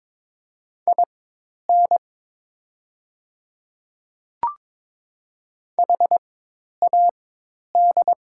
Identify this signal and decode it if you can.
Morse code, heard as a keyed tone, about 22 wpm, spaced as for 10 wpm, 690 Hz. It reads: IN HAD
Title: 4x Character Spacing - Top 300 Words - 22wpm